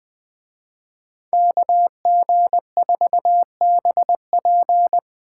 Morse code, heard as a keyed tone, about 20 wpm, 700 Hz. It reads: KG4BP